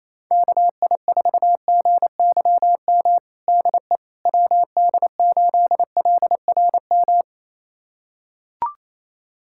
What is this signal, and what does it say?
Morse code, heard as a keyed tone, about 28 wpm, 700 Hz. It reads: KI4GYM DE WD8LRM
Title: USA Callsign DE Callsign - 6 Characters - 28wpm